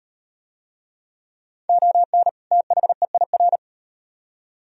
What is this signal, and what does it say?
Morse code, heard as a keyed tone, about 35 wpm, 700 Hz. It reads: ON THEIR